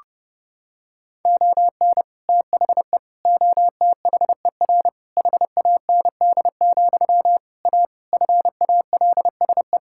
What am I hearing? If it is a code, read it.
Morse code, heard as a keyed tone, about 32 wpm, 700 Hz. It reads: ON THE OTHER HAND, A FALSE
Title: Sentences from Top 1600 Words 30wpm